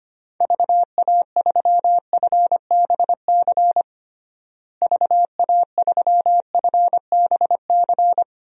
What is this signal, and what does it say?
Morse code, heard as a keyed tone, about 25 wpm, 700 Hz. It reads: VA3FBC VA3FBC
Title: DXCC-100 Call Signs - 6 Characters - Repeated 2x - 25wpm